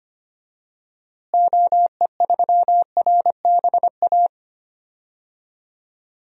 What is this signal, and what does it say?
Morse code, heard as a keyed tone, about 25 wpm, 700 Hz. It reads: OE3RBA
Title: DXCC-100 Call Signs - 6 Characters - Repeated 3x - 25wpm